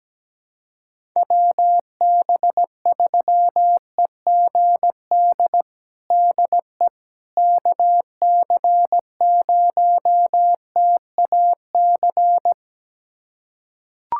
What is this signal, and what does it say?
Morse code, heard as a keyed tone, about 17 wpm, 700 Hz. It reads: WB3EGD DE KC0TAC
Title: USA Callsign DE Callsign - 6 Characters - 17wpm